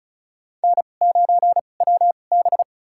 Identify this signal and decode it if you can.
Morse code, heard as a keyed tone, about 36 wpm, spaced as for 25 wpm, 700 Hz. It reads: N9WB
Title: USA Call Signs - 4 Characters - 2x Character Spacing - 35wpm